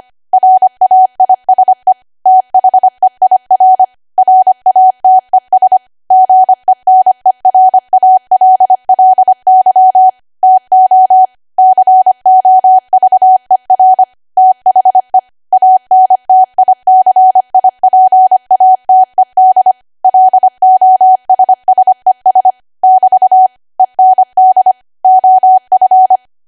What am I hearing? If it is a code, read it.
Morse code, heard as a keyed tone, about 25 wpm, 750 Hz. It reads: RAISE THEIR RATES GENERALLY TO COVER THE ANTICIPATED LOSSES <BT> END OF
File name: cwa / W1AW-Level3 / 25WPM